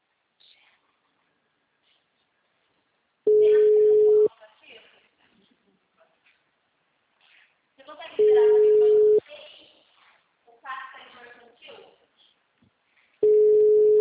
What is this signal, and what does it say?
Ringback tone (Europe, 425 Hz), with speech.
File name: out- -2020